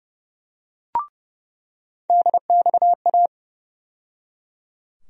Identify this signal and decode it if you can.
Morse code, heard as a keyed tone, about 30 wpm, 700 Hz. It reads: DXA
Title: USA Call Sign Postfixes - 3 Letters 30wpm